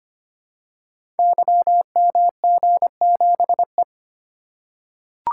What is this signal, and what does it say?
Morse code, heard as a keyed tone, about 25 wpm, 700 Hz. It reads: YMG7E
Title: Code Groups - 5 Letter:Numbers - 25wpm